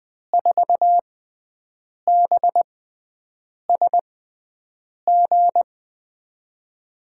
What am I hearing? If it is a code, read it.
Morse code, heard as a keyed tone, about 20 wpm, spaced as for 7 wpm, 700 Hz. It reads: 4BSG
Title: USA Call Signs - 6 Characters - 6x Character Spacing - 20wpm